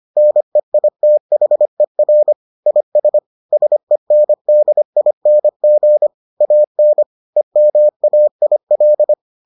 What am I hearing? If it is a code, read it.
Morse code, heard as a keyed tone, about 25 wpm, 600 Hz. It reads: NEITHER IS SENDING AN EMAIL